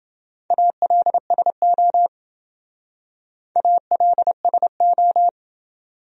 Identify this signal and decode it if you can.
Morse code, heard as a keyed tone, about 28 wpm, 700 Hz. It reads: ALSO ALSO